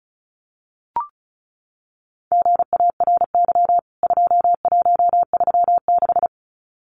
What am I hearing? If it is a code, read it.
Morse code, heard as a keyed tone, about 34 wpm, 700 Hz. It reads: GARY 2136
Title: CWT Exchange 35wpm